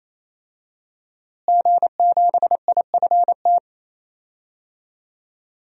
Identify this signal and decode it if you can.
Morse code, heard as a keyed tone, about 28 wpm, 700 Hz. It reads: G7IFT
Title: DXCC-100 Call Signs - 5 Characters - Repeated 3x - 28wpm